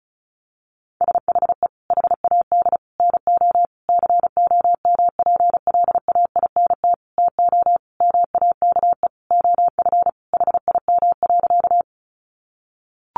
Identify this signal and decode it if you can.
Morse code, heard as a keyed tone, about 34 wpm, 700 Hz. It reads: SHE HAD NO COMPLAINT TO MAKE OF HIM.
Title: Sentences from Top 1400 Words 35wpm